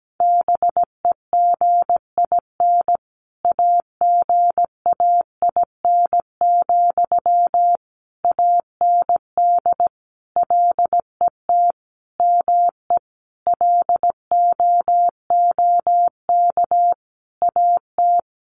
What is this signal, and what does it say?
Morse code, heard as a keyed tone, about 17 wpm, 700 Hz. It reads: BEGIN AGAIN, AND LET ME LOOK AT
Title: Sentences from Top 800 Words 17wpm